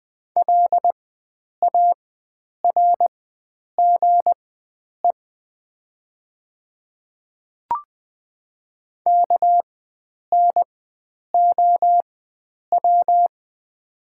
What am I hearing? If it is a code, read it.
Morse code, heard as a keyed tone, about 20 wpm, spaced as for 9 wpm, 700 Hz. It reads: LARGE KNOW
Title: 4x Character Spacing - Top 300 Words - 20wpm